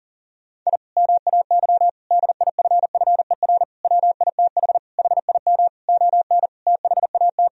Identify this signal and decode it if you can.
Morse code, heard as a keyed tone, about 40 wpm, 700 Hz. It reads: I MAY DIFFER WITH HIM ON THAT
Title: Sentences from Top 1000 Words - Review Part 1 - 40wpm